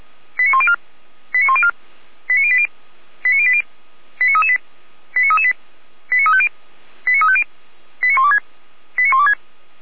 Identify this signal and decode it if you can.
Five-tone selective call (ZVEI), heard as quick runs of five tones, tones of about 70 ms. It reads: 89104 89104 89080 89080 89338 89338 89350 89350 89126 89126